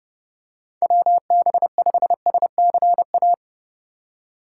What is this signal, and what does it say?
Morse code, heard as a keyed tone, about 30 wpm, 700 Hz. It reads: WB5SCA
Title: USA Call Signs - 3x Character Spacing - 30wpm